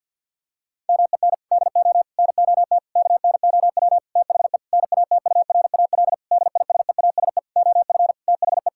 Morse code, heard as a keyed tone, about 49 wpm, 700 Hz. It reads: MEN DO NOT KNOW THE NATURAL DISEASE OF THE
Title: Sentences from Top 1000 Words - Review Part 6 - 50wpm